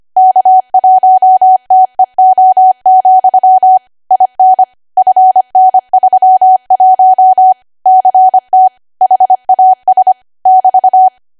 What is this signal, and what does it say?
Morse code, heard as a keyed tone, about 25 wpm, 750 Hz. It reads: K1TEO, IN FN31 CT HAS <BT>